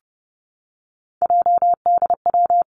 Morse code, heard as a keyed tone, about 30 wpm, 700 Hz. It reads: JDW
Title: Three Challenging Letter-Number Combinations 30wpm